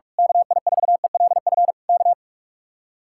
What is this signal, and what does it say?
Morse code, heard as a keyed tone, about 46 wpm, 700 Hz. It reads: KI4ELF K
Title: Speed-Racing Common QSO Phrases Encore - 45wpm